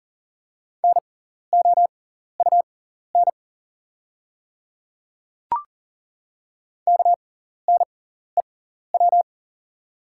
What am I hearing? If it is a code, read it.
Morse code, heard as a keyed tone, about 39 wpm, spaced as for 14 wpm, 700 Hz. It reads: NOUN KNEW
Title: 6x Character Spacing - Top 500 Words - 40wpm